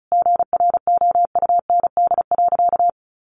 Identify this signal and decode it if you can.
Morse code, heard as a keyed tone, about 34 wpm, 700 Hz. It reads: GROUND.
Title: Sentences from Top 1100 Words 35wpm